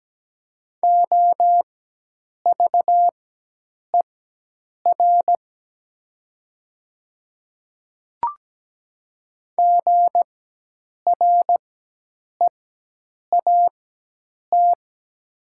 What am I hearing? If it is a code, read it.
Morse code, heard as a keyed tone, about 17 wpm, spaced as for 8 wpm, 700 Hz. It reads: OVER GREAT
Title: 4x Character Spacing - Top 200 Words - 17wpm